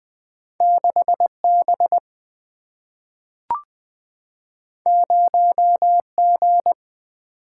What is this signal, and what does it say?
Morse code, heard as a keyed tone, about 20 wpm, 700 Hz. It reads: 6 B 0 G
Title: Single-Letter Single-Number 20wpm